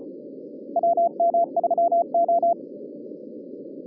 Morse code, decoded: WM3O